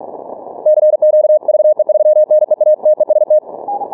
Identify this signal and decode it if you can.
Morse code, heard as a keyed tone, about 43 wpm, 610 Hz. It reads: CQ VE3NEA TEST
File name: training / morserunner / Example